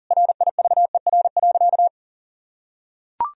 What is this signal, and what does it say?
Morse code, heard as a keyed tone, about 40 wpm, 700 Hz. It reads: RIVER.
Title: Sentences from Top 1000 Words - Review Part 1 - 40wpm